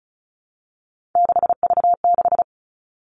Morse code, heard as a keyed tone, about 36 wpm, 700 Hz. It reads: 6V6
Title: Three Challenging Letter-Number Combinations 35wpm